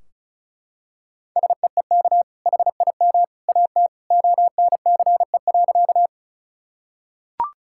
Morse code, heard as a keyed tone, about 34 wpm, 700 Hz. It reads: SEEK HIM AT ONCE.
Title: Sentences from Top 1600 Words 35wpm